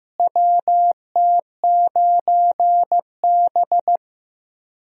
Morse code, heard as a keyed tone, about 15 wpm, 700 Hz. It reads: WT9B
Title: USA Call Signs - 4 Characters - Repeated 3x - 15wpm